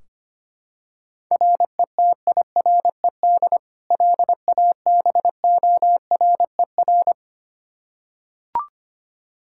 Morse code, heard as a keyed tone, about 25 wpm, 700 Hz. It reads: RETIRED LABORER
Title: Practice QSO - Exchange 3 - More Info - 25wpm